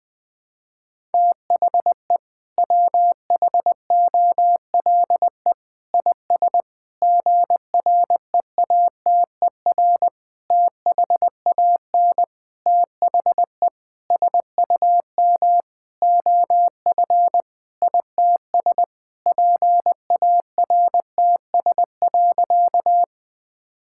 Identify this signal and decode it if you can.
Morse code, heard as a keyed tone, about 20 wpm, 700 Hz. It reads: THE WHOLE IS GREATER THAN THE SUM OF ITS PARTS.
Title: Sentences with 6-Word Phrases - 20wpm